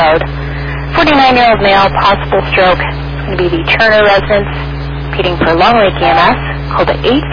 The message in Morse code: none

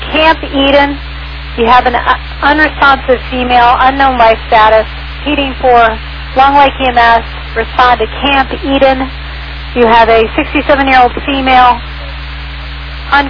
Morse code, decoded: none

